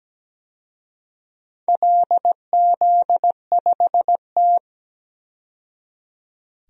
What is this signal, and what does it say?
Morse code, heard as a keyed tone, about 17 wpm, 700 Hz. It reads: LZ5T
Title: DXCC-100 Call Signs - 4 Characters - 17wpm